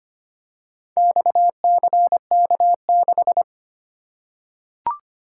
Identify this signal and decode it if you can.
Morse code, heard as a keyed tone, about 25 wpm, 700 Hz. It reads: XCK6
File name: practice / 25wpm / Four Letter-Number Combinations - 25wpm